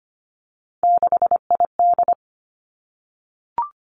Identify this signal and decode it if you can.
Morse code, heard as a keyed tone, about 25 wpm, 700 Hz. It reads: 6ID
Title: Three Challenging Letter-Number Combinations 25wpm